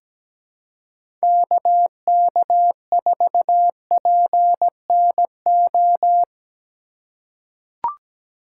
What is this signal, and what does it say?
Morse code, heard as a keyed tone, about 17 wpm, 700 Hz. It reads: KK4PNO
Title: USA Call Signs - 6 Characters - 3x Character Spacing - 17wpm